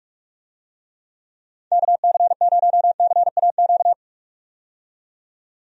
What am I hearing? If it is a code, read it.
Morse code, heard as a keyed tone, about 46 wpm, 700 Hz. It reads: KC0CAQ
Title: USA Call Signs - 6 Characters - 45wpm